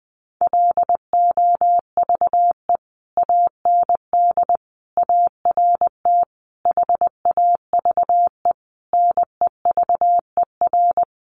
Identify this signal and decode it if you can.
Morse code, heard as a keyed tone, about 20 wpm, 700 Hz. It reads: LOVE AND ART HAVE NEVER
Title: Sentences from Top 1100 Words 20wpm